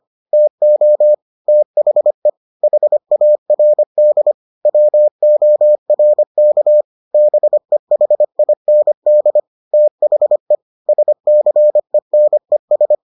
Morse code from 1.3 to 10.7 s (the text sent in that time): THE HARD WORK BEHIND THE